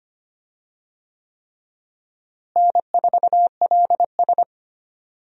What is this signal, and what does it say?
Morse code, heard as a keyed tone, about 25 wpm, 700 Hz. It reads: N4LS
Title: USA Call Signs - 4 Characters - Repeated 3x - 25wpm